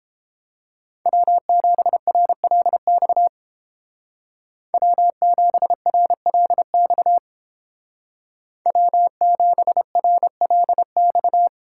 Morse code, heard as a keyed tone, about 29 wpm, 700 Hz. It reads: W7RLX W7RLX W7RLX